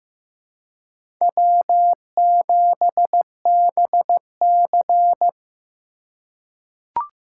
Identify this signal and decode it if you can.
Morse code, heard as a keyed tone, about 15 wpm, 700 Hz. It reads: W7BC
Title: USA Call Signs - 4 Characters - 8x Character Spacing - 15wpm